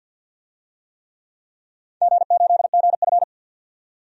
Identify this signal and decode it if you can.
Morse code, heard as a keyed tone, about 46 wpm, 700 Hz. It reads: G8GF
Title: DXCC-100 Call Signs - 4 Characters - 50wpm